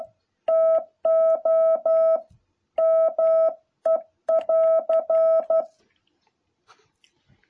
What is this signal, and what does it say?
Morse code, heard as a keyed tone, about 12 wpm, 650 Hz. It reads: TO ME<AR>
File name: jh8cbh / 2021cw